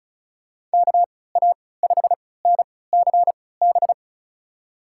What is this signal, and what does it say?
Morse code, heard as a keyed tone, about 34 wpm, spaced as for 20 wpm, 700 Hz. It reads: KA5NCB
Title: USA Call Signs - 6 Characters - 3x Character Spacing - 35wpm